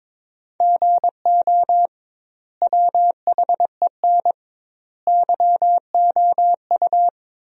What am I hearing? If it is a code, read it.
Morse code, heard as a keyed tone, about 22 wpm, 700 Hz. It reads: GO WHEN YOU
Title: 2x Word-Spacing - Sentences from Top 500 Words - 22wpm